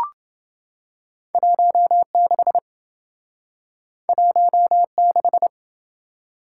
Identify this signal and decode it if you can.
Morse code, heard as a keyed tone, about 28 wpm, 700 Hz. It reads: 16 16